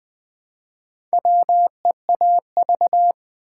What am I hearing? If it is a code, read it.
Morse code, heard as a keyed tone, about 20 wpm, 700 Hz. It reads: WEAV
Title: Build A Word - 1 Letter At A Time - 5-Letter Words - 20wpm